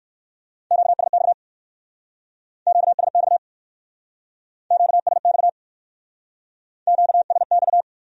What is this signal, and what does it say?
Morse code, heard as a keyed tone, about 58 wpm, 700 Hz. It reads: QSX QSX QSX QSX